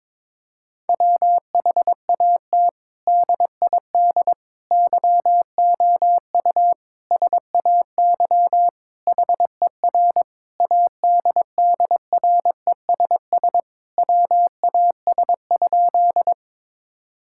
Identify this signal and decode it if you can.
Morse code, heard as a keyed tone, about 22 wpm, 700 Hz. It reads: WHAT DID YOU SAY HER ADDRESS WAS?
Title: Sentences from Top 1000 Words - Review Part 6 - 22wpm